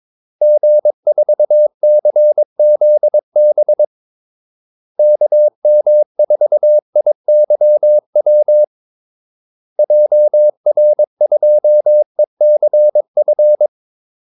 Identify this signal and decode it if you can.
Morse code, heard as a keyed tone, about 22 wpm, 600 Hz. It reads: G4CZB KM4IYW JR2ECF
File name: Callsigns-22wpm